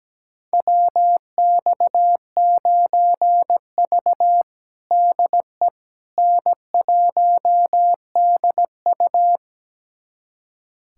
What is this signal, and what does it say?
Morse code, heard as a keyed tone, about 17 wpm, 700 Hz. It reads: WX9V DE N1DU
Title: USA Callsign DE Callsign - 4 Characters - 17wpm